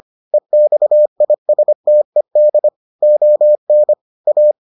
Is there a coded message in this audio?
Yes: Morse code from 0.3 to 4.5 s, heard as a keyed tone, about 25 wpm, 600 Hz. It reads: EXISTED ON A